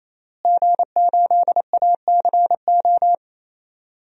Morse code, heard as a keyed tone, about 27 wpm, 700 Hz. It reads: G8ACO
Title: DXCC-100 Call Signs - 5 Characters - Repeated 3x - 28wpm